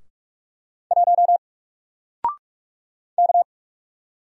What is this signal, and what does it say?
Morse code, heard as a keyed tone, about 44 wpm, 700 Hz. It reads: 1 K
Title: Single Letter-Number Alt 45wpm